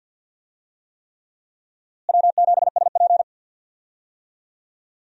Morse code, heard as a keyed tone, about 52 wpm, 700 Hz. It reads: W7SP
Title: USA Call Signs - 4 Characters - Repeated 3x - 50wpm